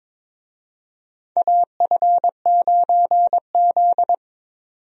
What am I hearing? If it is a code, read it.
Morse code, heard as a keyed tone, about 21 wpm, 700 Hz. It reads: AF9Z
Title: USA Call Signs - 4 Characters - 22wpm